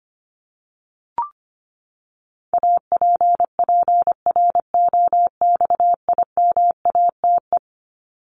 Morse code, heard as a keyed tone, about 25 wpm, 700 Hz. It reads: APPROXIMATE
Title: LY Suffix - Top 100 Words with Base Words 25wpm